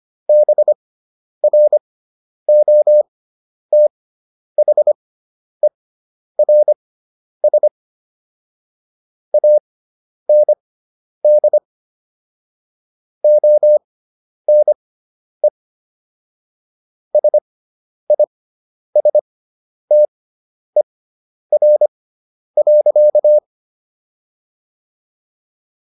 Morse code, heard as a keyed tone, about 25 wpm, spaced as for 10 wpm, 600 Hz. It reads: BROTHERS AND ONE SISTER.